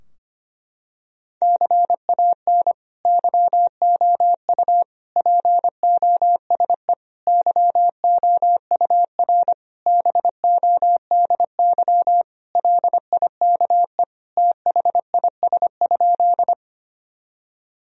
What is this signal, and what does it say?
Morse code, heard as a keyed tone, about 25 wpm, 700 Hz. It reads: CAN YOU POSE YOUR BODY LIKE THIS?
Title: Sentences from Top 1000 Words - Review Part 2 - 25wpm